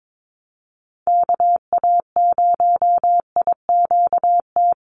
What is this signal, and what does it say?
Morse code, heard as a keyed tone, about 22 wpm, 700 Hz. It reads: KA0IQT
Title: Call Signs - 22wpm